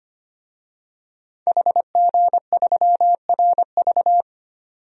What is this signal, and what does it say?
Morse code, heard as a keyed tone, about 25 wpm, 700 Hz. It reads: HG3RV
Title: DXCC-100 Call Signs - 5 Characters - Repeated 2x - 25wpm